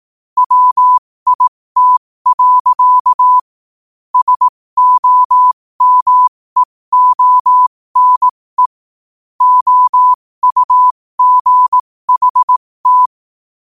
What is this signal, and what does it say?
Morse code, heard as a keyed tone, about 18 wpm, spaced as for 15 wpm, 1000 Hz. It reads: WIT. SOMEONE OUGHT
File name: qotdfiles / 15wpm